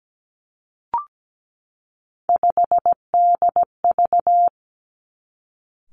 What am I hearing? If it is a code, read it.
Morse code, heard as a keyed tone, about 17 wpm, 700 Hz. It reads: 5DV
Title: Three Challenging Letter-Number Combinations 17wpm